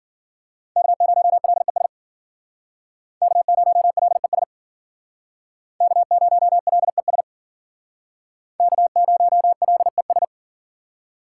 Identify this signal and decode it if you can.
Morse code, heard as a keyed tone, about 51 wpm, 700 Hz. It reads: K0LES K0LES K0LES K0LES